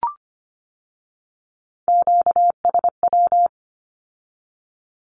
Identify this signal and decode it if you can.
Morse code, heard as a keyed tone, about 25 wpm, 700 Hz. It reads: QSW
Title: Q-Codes with Spoken Meaning - 25wpm